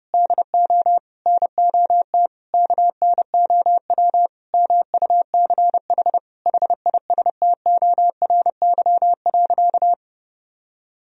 Morse code, heard as a keyed tone, about 32 wpm, 700 Hz. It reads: DO NOT KNOW MUCH HISTORY.
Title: Sentences from Top 1000 Words - Review Part 2 - 30wpm